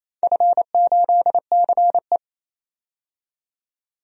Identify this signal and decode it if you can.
Morse code, heard as a keyed tone, about 28 wpm, 700 Hz. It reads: F8CE
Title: DXCC-100 Call Signs - 4 Characters - 28wpm